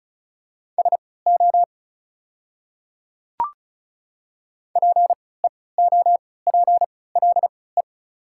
Morse code, heard as a keyed tone, about 36 wpm, spaced as for 20 wpm, 690 Hz. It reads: SO PEOPLE
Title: 3x Character Spacing - Top 100 Words - 35wpm